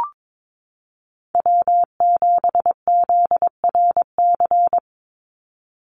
Morse code, heard as a keyed tone, about 22 wpm, 700 Hz. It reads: W7ZRC
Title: Call Signs Repeated 3x - 22wpm